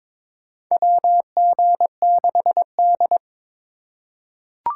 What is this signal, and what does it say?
Morse code, heard as a keyed tone, about 22 wpm, 700 Hz. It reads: WG6D
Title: USA Call Signs - 4 Characters - 3x Character Spacing - 22wpm